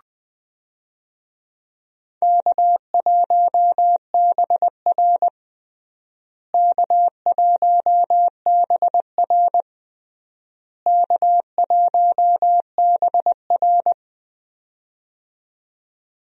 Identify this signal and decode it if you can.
Morse code, heard as a keyed tone, about 21 wpm, 700 Hz. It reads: K1BR K1BR K1BR
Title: USA Call Signs - 4 Characters - Repeated 3x - 20wpm